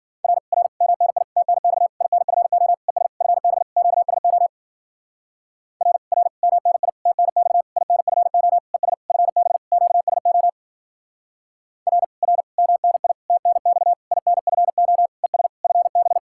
Morse code, heard as a keyed tone, about 50 wpm, 700 Hz. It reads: R R MNI TNX INFO ES FB QSO R R MNI TNX INFO ES FB QSO R R MNI TNX INFO ES FB
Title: Speed-Racing Common QSO Phrases - 40wpm